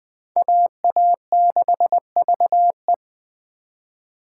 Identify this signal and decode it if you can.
Morse code, heard as a keyed tone, about 20 wpm, 700 Hz. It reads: AA6VE